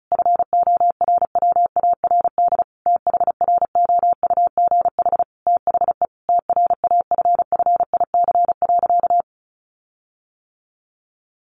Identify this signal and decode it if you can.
Morse code, heard as a keyed tone, about 34 wpm, 700 Hz. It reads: FORWARD THROUGH THE TRAFFIC.
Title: Sentences from Top 1300 Words 35wpm